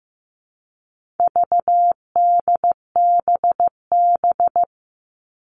Morse code, heard as a keyed tone, about 15 wpm, 700 Hz. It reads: VDBB
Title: All Cut Numbers - 4-Digit 15wpm